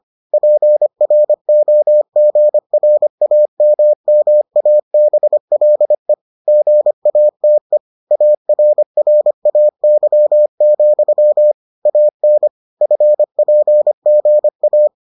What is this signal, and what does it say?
Morse code, heard as a keyed tone, about 25 wpm, 600 Hz. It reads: PROGRAMMABLE GATE ARRAY, AN FPGA